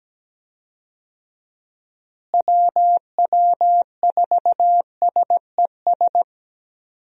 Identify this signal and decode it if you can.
Morse code, heard as a keyed tone, about 17 wpm, 700 Hz. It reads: WW4SES
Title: USA Call Signs - 6 Characters - Repeated 2x - 17wpm